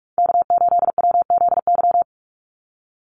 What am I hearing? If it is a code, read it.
Morse code, heard as a keyed tone, about 44 wpm, 700 Hz. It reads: K8WZY
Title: USA Call Signs 45wpm